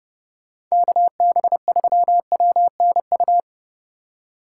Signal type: Morse code

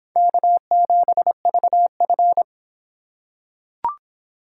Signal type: Morse code

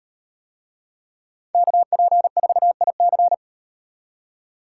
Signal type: Morse code

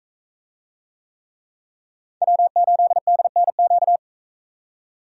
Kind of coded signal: Morse code

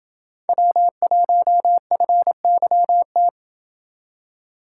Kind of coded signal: Morse code

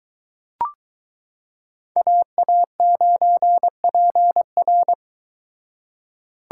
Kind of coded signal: Morse code

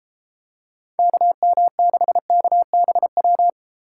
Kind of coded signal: Morse code

